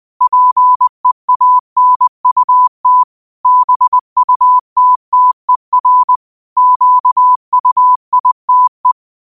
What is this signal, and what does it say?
Morse code, heard as a keyed tone, about 20 wpm, 1000 Hz. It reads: PEANUT BUTTER QUITE
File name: qotdfiles / 20wpm